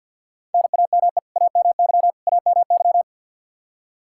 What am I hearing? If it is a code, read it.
Morse code, heard as a keyed tone, about 51 wpm, 700 Hz. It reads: NAME AMY AMY
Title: Practice QSO - Exchange 1 - Beginning - 50wpm